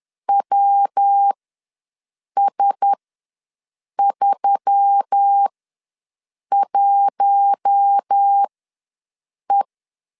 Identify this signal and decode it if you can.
Morse code, heard as a keyed tone, about 10 wpm, spaced as for 6 wpm, 800 Hz. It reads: WS31E
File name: LEZIONI / 5 LBVC / 5 L ws 10wpm 3636